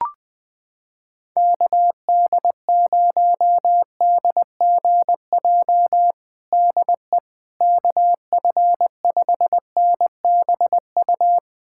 Morse code, heard as a keyed tone, about 20 wpm, 700 Hz. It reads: KD0DGJ DE KF5NBU
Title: USA Callsign DE Callsign - 6 Characters - 20wpm